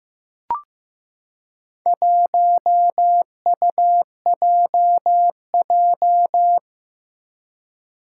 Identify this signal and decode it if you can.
Morse code, heard as a keyed tone, about 15 wpm, 700 Hz. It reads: 1UJJ
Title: Four Challenging Letter-Number Combinations 15wpm